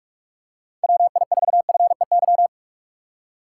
Morse code, heard as a keyed tone, about 46 wpm, 700 Hz. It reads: WI4FEY
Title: USA Call Signs - 4x Character Spacing - 45wpm